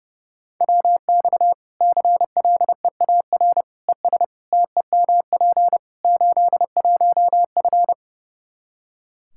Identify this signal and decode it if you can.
Morse code, heard as a keyed tone, about 30 wpm, 700 Hz. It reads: WX CLEAR ES TEMP 81F
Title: Practice QSO - Exchange 2 - Info - 30wpm